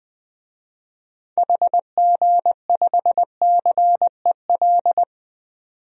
Morse code, heard as a keyed tone, about 21 wpm, 700 Hz. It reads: HG5CEL
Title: DXCC-100 Call Signs - 6 Characters - 20wpm